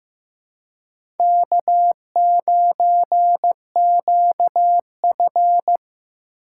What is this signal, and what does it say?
Morse code, heard as a keyed tone, about 15 wpm, 700 Hz. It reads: K9QF
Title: USA Call Signs - 4 Characters - 4x Character Spacing - 15wpm